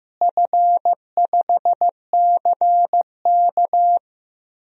Morse code, heard as a keyed tone, about 15 wpm, 700 Hz. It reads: F5CK
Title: DXCC-100 Call Signs - 4 Characters - Repeated 3x - 15wpm